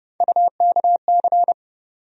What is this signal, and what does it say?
Morse code, heard as a keyed tone, about 30 wpm, 700 Hz. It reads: UKC
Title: USA Call Sign Postfixes - 3 Letters 30wpm